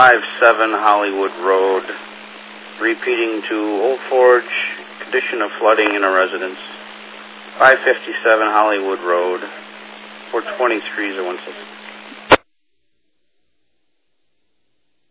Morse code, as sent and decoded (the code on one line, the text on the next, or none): none